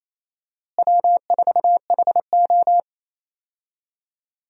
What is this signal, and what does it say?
Morse code, heard as a keyed tone, about 28 wpm, 700 Hz. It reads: W4HO
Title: USA Call Signs - 4 Characters - Repeated 2x - 28wpm